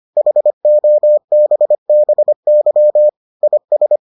Morse code, heard as a keyed tone, about 25 wpm, 600 Hz. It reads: HOBBY IS